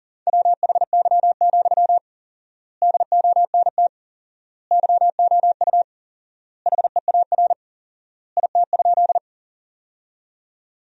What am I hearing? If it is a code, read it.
Morse code, heard as a keyed tone, about 40 wpm, 700 Hz. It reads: WHY, DONT YOU HEAR IT?
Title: 4x Word-Spacing - Sentences from Top 300 Words - 40wpm